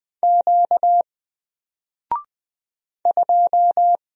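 Morse code, heard as a keyed tone, about 20 wpm, 700 Hz. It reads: Q 2
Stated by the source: Single letter and number Morse drill at 20 wpm, from N4LCD.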